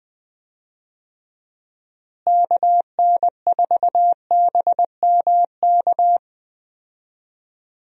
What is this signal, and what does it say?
Morse code, heard as a keyed tone, about 21 wpm, 700 Hz. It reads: KN4BMK
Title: USA Call Signs - 6 Characters - Repeated 3x - 20wpm